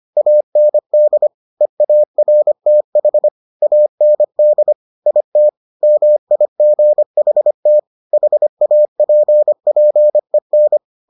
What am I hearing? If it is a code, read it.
Morse code, heard as a keyed tone, about 25 wpm, 600 Hz. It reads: AND EARTH AND IT MIGHT HAPPEN